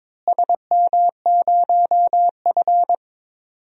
Morse code, heard as a keyed tone, about 22 wpm, 700 Hz. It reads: SM0F
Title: DXCC-100 Call Signs - 4 Characters - Repeated 2x - 22wpm